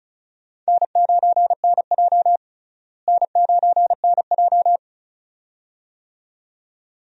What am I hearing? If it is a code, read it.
Morse code, heard as a keyed tone, about 34 wpm, 700 Hz. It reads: N9NJ N9NJ
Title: USA Call Signs - 4 Characters - Repeated 2x - 35wpm